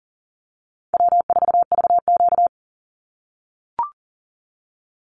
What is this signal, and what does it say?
Morse code, heard as a keyed tone, about 41 wpm, 700 Hz. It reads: W4VQ
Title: Easy Call Signs - 40wpm